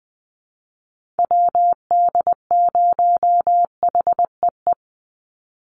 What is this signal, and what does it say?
Morse code, heard as a keyed tone, about 20 wpm, 700 Hz. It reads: WD0HEE
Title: USA Call Signs 20wpm